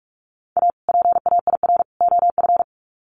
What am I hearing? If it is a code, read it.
Morse code, heard as a keyed tone, about 44 wpm, 700 Hz. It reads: A PAIR OF
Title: Sets of 3 Words 45wpm